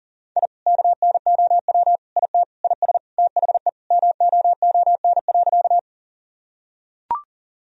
Morse code, heard as a keyed tone, about 37 wpm, 700 Hz. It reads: I KNOW IT IS THE MOON.
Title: Sentences from Top 1000 Words - Review Part 3 - 40wpm